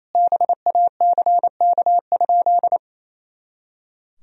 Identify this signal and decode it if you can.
Morse code, heard as a keyed tone, about 28 wpm, 700 Hz. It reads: BACK?
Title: Sentences from Top 300 Words and Abbreviations 28wpm